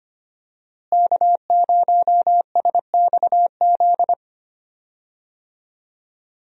Morse code, heard as a keyed tone, about 25 wpm, 700 Hz. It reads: K0SXZ
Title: USA Call Signs - 5 Characters - Repeated 3x - 25wpm